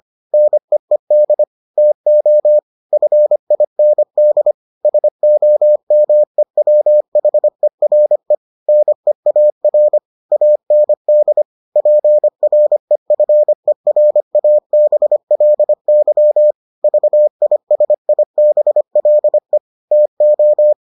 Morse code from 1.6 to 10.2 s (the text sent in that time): TO FIND SOMEWHERE NEAR